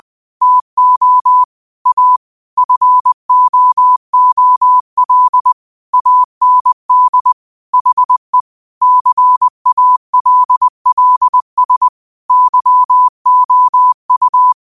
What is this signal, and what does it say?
Morse code, heard as a keyed tone, about 20 wpm, 1000 Hz. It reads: TO A FOOL AND HE CALLS YOU
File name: qotdfiles / 20wpm